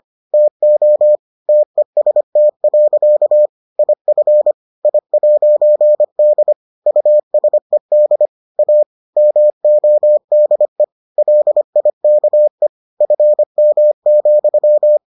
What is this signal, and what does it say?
Morse code, heard as a keyed tone, about 25 wpm, 600 Hz. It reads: TO TEST. IF I'D USED A MODE LIKE FM,